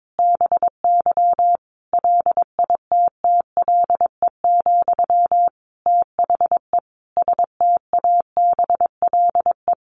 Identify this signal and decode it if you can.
Morse code, heard as a keyed tone, about 22 wpm, 700 Hz. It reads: BY LITTLE, THE STABLE
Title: Sentences from Top 1500 Words 22wpm